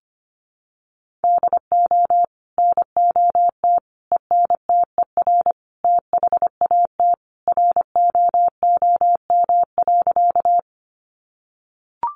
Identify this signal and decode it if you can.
Morse code, heard as a keyed tone, about 25 wpm, 700 Hz. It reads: DO NOT ENTER THAT ROOM.
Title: Sentences from Top 1300 Words 25wpm